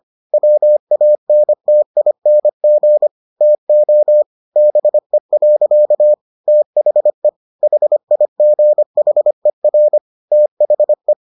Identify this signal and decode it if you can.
Morse code, heard as a keyed tone, about 25 wpm, 600 Hz. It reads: WANTING TO BE. THE HIGHER THE